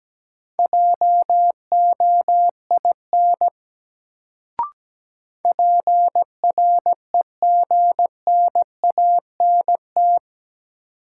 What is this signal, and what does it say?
Morse code, heard as a keyed tone, about 17 wpm, 700 Hz. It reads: JOIN PREGNANT
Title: Sentences from Top 1000 Words - Review Part 6 - New Word Drill - 17wpm